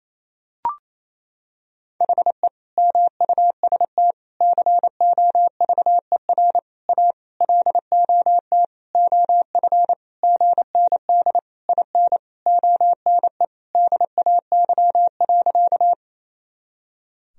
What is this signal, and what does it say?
Morse code, heard as a keyed tone, about 28 wpm, 700 Hz. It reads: HE MUST COVER A LOT OF GND IN ONE DAY.
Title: Sentences from Top 500 Words and Abbreviations 28wpm